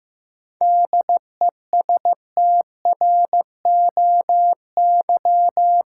Morse code, heard as a keyed tone, about 15 wpm, 700 Hz. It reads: DESTROY
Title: Seven Letter Words - 15wpm